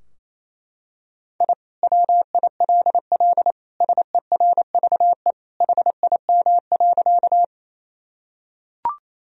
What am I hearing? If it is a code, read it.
Morse code, heard as a keyed tone, about 28 wpm, 700 Hz. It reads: I WILL SERVE HIM.
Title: Sentences from Top 1000 Words - Review Part 2 - 28wpm